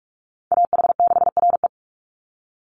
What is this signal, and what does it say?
Morse code, heard as a keyed tone, about 44 wpm, 700 Hz. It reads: AH6RE
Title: Call Signs - 45wpm